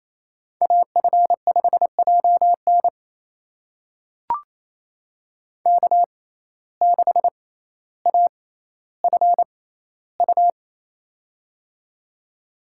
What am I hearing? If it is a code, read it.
Morse code, heard as a keyed tone, about 28 wpm, spaced as for 14 wpm, 700 Hz. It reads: AF5JN K6AFU
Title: USA Call Signs - 5 Characters - 6x Character Spacing - 28wpm